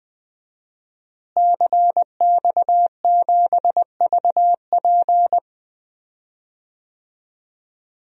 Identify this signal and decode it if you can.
Morse code, heard as a keyed tone, about 20 wpm, 700 Hz. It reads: CX7VP
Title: DXCC-100 Call Signs - 5 Characters - Repeated 2x - 20wpm